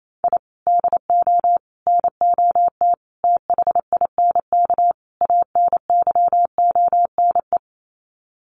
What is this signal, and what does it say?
Morse code, heard as a keyed tone, about 28 wpm, 700 Hz. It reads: I DO NOT THINK ANYONE
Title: Sets of 5 Words Encore 28wpm